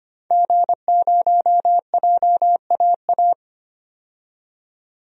Morse code, heard as a keyed tone, about 25 wpm, 700 Hz. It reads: G0JAA
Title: DXCC-100 Call Signs - 5 Characters - 25wpm